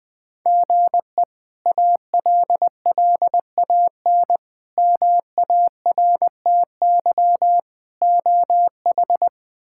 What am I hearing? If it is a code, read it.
Morse code, heard as a keyed tone, about 21 wpm, 700 Hz. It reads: GE ALLAN MARTY OH
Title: K1USN SST QSO Practice - 20wpm